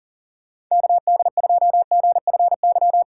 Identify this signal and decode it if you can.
Morse code, heard as a keyed tone, about 40 wpm, 700 Hz. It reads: KD2GFY
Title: USA Call Signs - 6 Characters - 2x Character Spacing - 40wpm